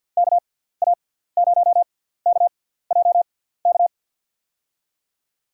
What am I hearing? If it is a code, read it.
Morse code, heard as a keyed tone, about 53 wpm, spaced as for 17 wpm, 700 Hz. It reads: KA0KJK
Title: USA Call Signs - 6 Characters - 6x Character Spacing - 50wpm